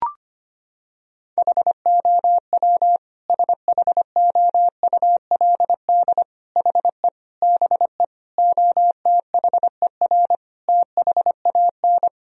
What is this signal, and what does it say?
Morse code, heard as a keyed tone, about 25 wpm, 700 Hz. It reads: HOW SHOULD HE BE OTHER THAN
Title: Sentences from Top 1000 Words - Review Part 2 - 25wpm